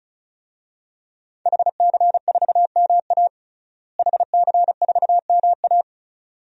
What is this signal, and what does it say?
Morse code, heard as a keyed tone, about 34 wpm, 700 Hz. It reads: HC4MA HC4MA
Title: DXCC-100 Call Signs - 5 Characters - Repeated 2x - 35wpm